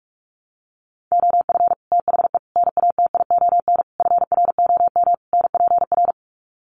Morse code, heard as a keyed tone, about 44 wpm, 700 Hz. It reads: OF THE NATION FROM NPR
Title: Sets of 5 Words 45wpm